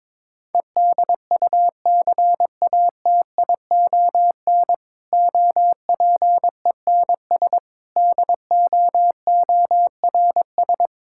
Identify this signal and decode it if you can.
Morse code, heard as a keyed tone, about 22 wpm, 700 Hz. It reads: EDUCATION OPENS DOORS
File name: Difficult Sentences with Hints - 22wpm